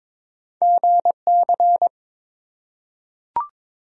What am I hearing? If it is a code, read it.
Morse code, heard as a keyed tone, about 22 wpm, 700 Hz. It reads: GC
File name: practice / 22wpm / Two Letter-Number Combinations - 22wpm